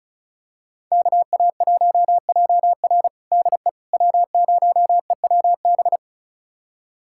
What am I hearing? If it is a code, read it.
Morse code, heard as a keyed tone, about 34 wpm, 690 Hz. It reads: KA1JR DE W0EWB
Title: USA Callsign DE Callsign - 5 Characters - 35wpm